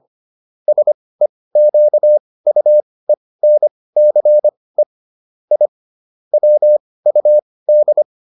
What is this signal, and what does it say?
Morse code, heard as a keyed tone, about 25 wpm, spaced as for 18 wpm, 600 Hz. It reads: SEQUENCE I WUD